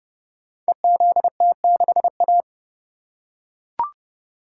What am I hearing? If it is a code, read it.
Morse code, heard as a keyed tone, about 30 wpm, 700 Hz. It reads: EZT6A
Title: Code Groups - 5 Letter:Numbers - 30wpm